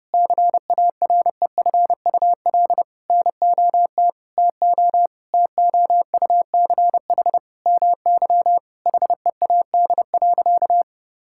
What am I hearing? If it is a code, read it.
Morse code, heard as a keyed tone, about 30 wpm, 700 Hz. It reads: CAREFUL NOT TO TOUCH MY HEAD.
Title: Sentences from Top 1600 Words 30wpm